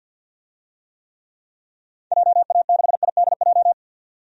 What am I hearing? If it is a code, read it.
Morse code, heard as a keyed tone, about 51 wpm, 700 Hz. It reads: JA6IDJ
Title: DXCC-100 Call Signs - 6 Characters - 50wpm